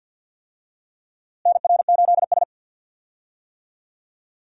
Morse code, heard as a keyed tone, about 52 wpm, 700 Hz. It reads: NR8S
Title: USA Call Signs - 4 Characters - Repeated 2x - 50wpm